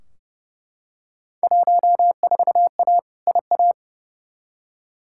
Morse code, heard as a keyed tone, about 30 wpm, 700 Hz. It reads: 14A IA
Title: Field Day Exchanges Encore - 30wpm